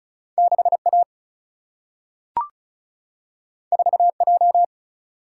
Morse code, heard as a keyed tone, about 36 wpm, 700 Hz. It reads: 6 A 4 J
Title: Single-Letter Single-Number 35wpm